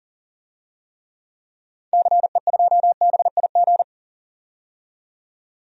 Morse code, heard as a keyed tone, about 40 wpm, 700 Hz. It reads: CE2BIG
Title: DXCC-100 Call Signs - 6 Characters - 40wpm